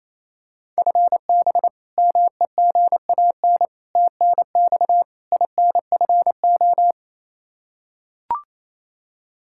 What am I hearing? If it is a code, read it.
Morse code, heard as a keyed tone, about 28 wpm, 700 Hz. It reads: FB MEGAN TNX INFO
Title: Practice QSO - Exchange 3 - More Info - 28wpm